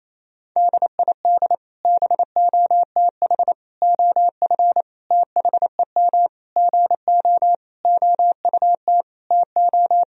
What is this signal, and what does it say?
Morse code, heard as a keyed tone, about 28 wpm, 700 Hz. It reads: DID BOTH OF THEM GO OUT TO
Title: Sentences from Top 1000 Words - Review Part 2 - 28wpm